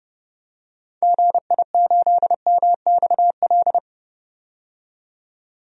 Morse code, heard as a keyed tone, about 30 wpm, 700 Hz. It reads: GI8MXL
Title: DXCC-100 Call Signs - 6 Characters - 30wpm